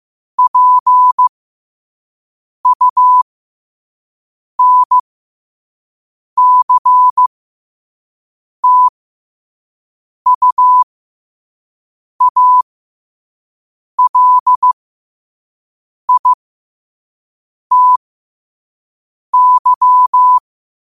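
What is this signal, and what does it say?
Morse code, heard as a keyed tone, about 15 wpm, spaced as for 5 wpm, 1000 Hz. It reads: PUNCTUALITY
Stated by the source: Morse code practice at 5 wpm.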